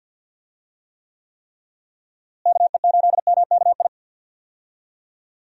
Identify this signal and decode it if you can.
Morse code, heard as a keyed tone, about 52 wpm, 700 Hz. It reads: KE8MKI
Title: USA Call Signs - 6 Characters - Repeated 3x - 50wpm